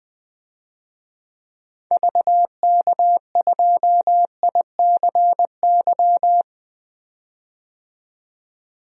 Morse code, heard as a keyed tone, about 21 wpm, 700 Hz. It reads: VK2ICY
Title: DXCC-100 Call Signs - 6 Characters - 20wpm